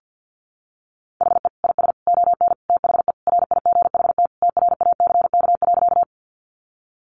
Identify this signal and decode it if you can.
Morse code, heard as a keyed tone, about 48 wpm, 700 Hz. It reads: HE IS ON THE RIGHT TRACK.
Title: Sentences from Top 1100 Words 50wpm